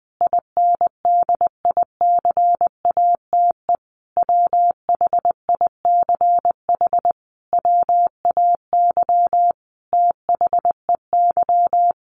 Morse code, heard as a keyed tone, about 20 wpm, 700 Hz. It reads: INDICATE WHICH WAY THEY
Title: Sentences from Top 1300 Words 20wpm